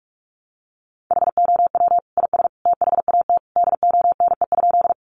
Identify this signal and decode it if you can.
Morse code, heard as a keyed tone, about 44 wpm, 700 Hz. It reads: HOW IS THAT DONE?
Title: Sentences from Top 500 Words 45wpm